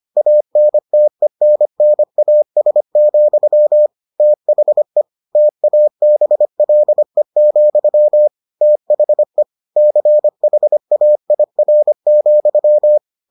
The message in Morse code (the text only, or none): ANTENNAS, THE TABLE, THE CHAIR,